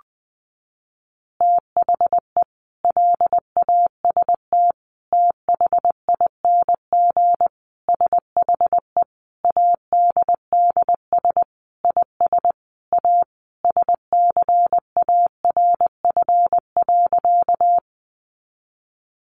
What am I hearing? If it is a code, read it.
Morse code, heard as a keyed tone, about 20 wpm, 700 Hz. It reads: THE LAST THING SHE ADDS IS A SCARF.
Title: Easy Phrases 20wpm